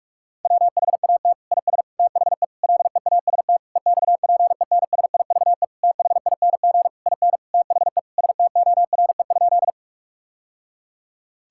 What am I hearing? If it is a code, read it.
Morse code, heard as a keyed tone, about 46 wpm, 700 Hz. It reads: WHAT IS THE LEAST EXPENSIVE THING IN THE STORE?
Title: Sentences from Top 1000 Words - Review Part 6 - 45wpm